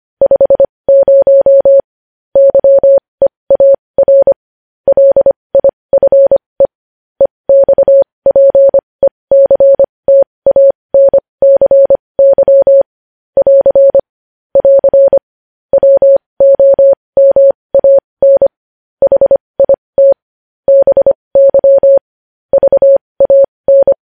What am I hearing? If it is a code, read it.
Morse code, heard as a keyed tone, about 25 wpm, spaced as for 20 wpm, 550 Hz. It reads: 50 YEAR LIFE EXPECTANCY <AR> <AR> WOMAN HIT BY VAN
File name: podcast / News headlines-1x-25x20